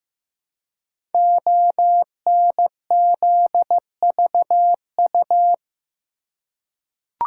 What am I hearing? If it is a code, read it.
Morse code, heard as a keyed tone, about 15 wpm, 700 Hz. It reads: ONZVU
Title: Code Groups - 5 Letter:Numbers - 15wpm